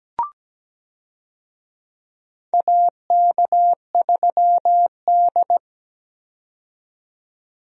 Morse code, heard as a keyed tone, about 17 wpm, 700 Hz. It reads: AK3D
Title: USA Call Signs - 4 Characters - 17wpm